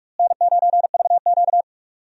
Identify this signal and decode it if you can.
Morse code, heard as a keyed tone, about 46 wpm, 700 Hz. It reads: N9VQ
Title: USA Call Signs - 4 Characters - Repeated 2x - 45wpm